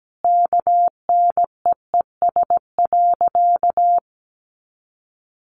Morse code, heard as a keyed tone, about 17 wpm, 700 Hz. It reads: KNEES.
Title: A Sound Of Thunder 17wpm